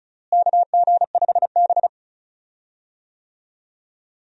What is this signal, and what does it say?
Morse code, heard as a keyed tone, about 36 wpm, 700 Hz. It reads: KG5B
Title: USA Call Signs - 4 Characters - Repeated 2x - 35wpm